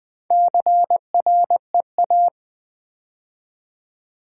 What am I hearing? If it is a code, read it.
Morse code, heard as a keyed tone, about 20 wpm, 700 Hz. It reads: CREA
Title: Build A Word - 4 Letters At A Time - 8-Letter Words - 20wpm